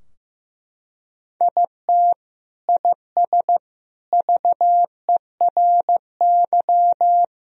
Morse code, heard as a keyed tone, about 15 wpm, 700 Hz. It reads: IT IS VERY
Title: Sentences from Top 1000 Words - Review Part 3 - 15wpm